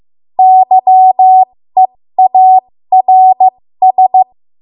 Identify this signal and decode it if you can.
Morse code, heard as a keyed tone, about 15 wpm, spaced as for 13 wpm, 750 Hz. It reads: YEARS
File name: Morse / 13WPM